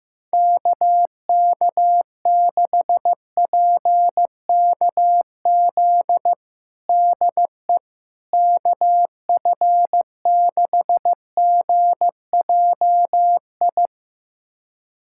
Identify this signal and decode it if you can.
Morse code, heard as a keyed tone, about 15 wpm, 700 Hz. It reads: KK6PKZ DE KF6GJI
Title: USA Callsign DE Callsign 15wpm